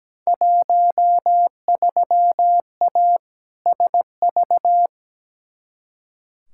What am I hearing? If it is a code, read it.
Morse code, heard as a keyed tone, about 17 wpm, 700 Hz. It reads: 13A SV
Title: Field Day Exchanges Encore - 17wpm